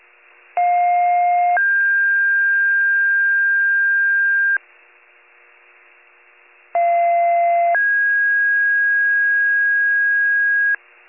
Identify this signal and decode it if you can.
Two-tone page, heard as a one-second tone, then a three-second tone, about 710 then 1600 Hz, about 690 then 1650 Hz. It is heard from 0.6 to 10.8 s.